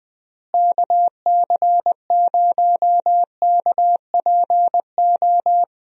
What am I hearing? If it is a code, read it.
Morse code, heard as a keyed tone, about 21 wpm, 700 Hz. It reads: KC0KPO
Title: USA Call Signs - 6 Characters - Repeated 3x - 20wpm